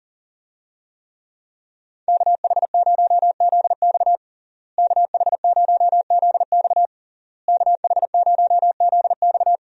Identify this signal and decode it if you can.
Morse code, heard as a keyed tone, about 40 wpm, 700 Hz. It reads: KH0ZX KH0ZX KH0ZX
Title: USA Call Signs - 5 Characters - Repeated 3x - 40wpm